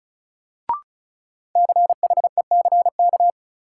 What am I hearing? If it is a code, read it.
Morse code, heard as a keyed tone, about 34 wpm, 700 Hz. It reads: CHECK
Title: Sentences from Top 1400 Words - New Word Drill Alt - 35wpm